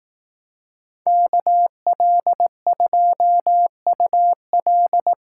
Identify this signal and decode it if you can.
Morse code, heard as a keyed tone, about 18 wpm, 700 Hz. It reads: KL2UL
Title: Speed Racing - USA Call Signs 15wpm